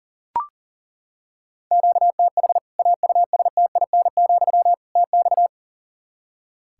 Morse code, heard as a keyed tone, about 37 wpm, 700 Hz. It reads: QTH AUSTIN, TX
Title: Practice QSO - Exchange 1 - Beginning - 40wpm